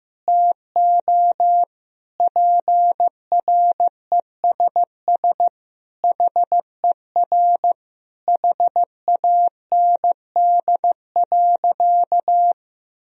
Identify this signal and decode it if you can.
Morse code, heard as a keyed tone, about 14 wpm, 700 Hz. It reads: TO PRESS HER HAND.
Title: Sentences from Top 1000 Words - Review Part 1 - 15wpm